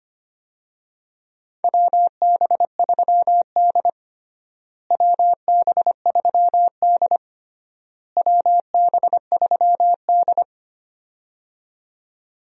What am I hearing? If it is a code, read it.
Morse code, heard as a keyed tone, about 25 wpm, 700 Hz. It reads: WB3D WB3D WB3D
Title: USA Call Signs - 4 Characters - Repeated 3x - 25wpm